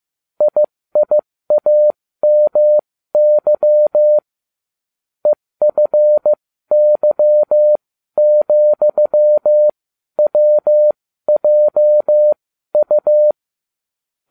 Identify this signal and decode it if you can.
Morse code, heard as a keyed tone, about 15 wpm, spaced as for 12 wpm, 600 Hz. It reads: IIAMY EFY,WJU